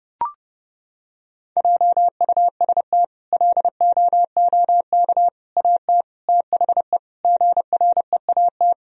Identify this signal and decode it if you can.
Morse code, heard as a keyed tone, about 30 wpm, 700 Hz. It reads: JUST LOOK AT THE GREAT
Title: Sentences from Top 1000 Words - Review Part 1 - 30wpm